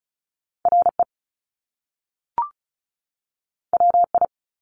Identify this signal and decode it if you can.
Morse code, heard as a keyed tone, about 36 wpm, 700 Hz. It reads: RE WI
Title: Top Two Letter Combinations in Words 35wpm